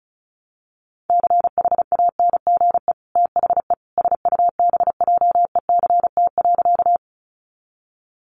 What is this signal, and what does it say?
Morse code, heard as a keyed tone, about 34 wpm, 700 Hz. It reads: CHANGE THE SUBJECT.
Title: Sentences from Top 800 Words 35wpm